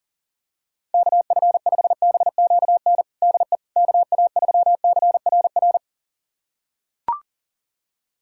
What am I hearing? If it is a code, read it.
Morse code, heard as a keyed tone, about 40 wpm, 700 Hz. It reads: KF5BQN DE KA3CRR
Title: USA Callsign DE Callsign - 6 Characters - 40wpm